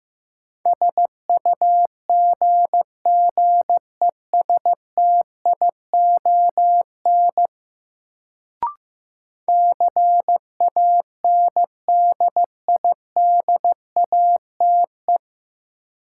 Morse code, heard as a keyed tone, about 15 wpm, 700 Hz. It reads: SUGGESTION CANDIDATE
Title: Sentences from Top 1600 Words - New Word Drill - 15wpm